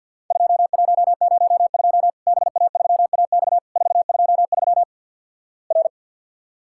Morse code, heard as a keyed tone, about 50 wpm, 700 Hz. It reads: 2102 BA3AX 423 R
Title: Worked All Europe DX Contest - QTC Exchange - 50wpm